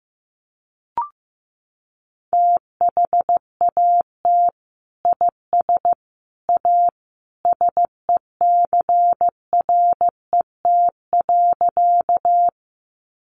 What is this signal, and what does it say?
Morse code, heard as a keyed tone, about 15 wpm, 700 Hz. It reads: THAT IS A SECRET.
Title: Sentences from Top 1200 Words 15wpm